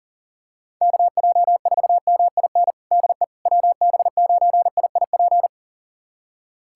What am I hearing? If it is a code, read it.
Morse code, heard as a keyed tone, about 40 wpm, 700 Hz. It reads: KJ4MIN DE WB9IIP
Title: USA Callsign DE Callsign - 6 Characters - 40wpm